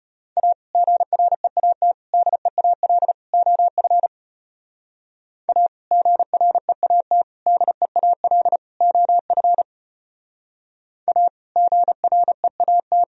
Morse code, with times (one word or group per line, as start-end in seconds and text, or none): 0.4-0.5 A
0.7-1.9 GREAT
2.1-3.1 DEAL
3.3-4.1 OF
5.5-5.7 A
5.9-7.2 GREAT
7.5-8.6 DEAL
8.8-9.6 OF
11.1-11.3 A
11.6-13.0 GREAT